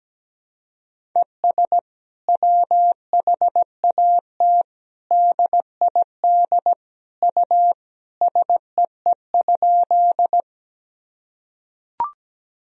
Morse code, heard as a keyed tone, about 17 wpm, 700 Hz. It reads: ES WHAT DID U SEE?
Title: Sentences from Top 100 Words and Abbreviations 17wpm